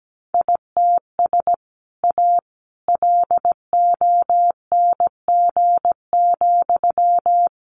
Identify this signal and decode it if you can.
Morse code, heard as a keyed tone, about 17 wpm, 700 Hz. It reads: ITS A LONG,
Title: Sentences from Top 600 Words 17wpm